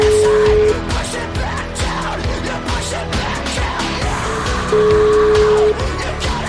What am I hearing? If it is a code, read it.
Ringback tone (Europe, 425 Hz), with noise.